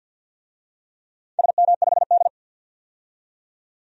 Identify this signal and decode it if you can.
Morse code, heard as a keyed tone, about 53 wpm, 700 Hz. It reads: SM5D
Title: DXCC-100 Call Signs - 4 Characters - 50wpm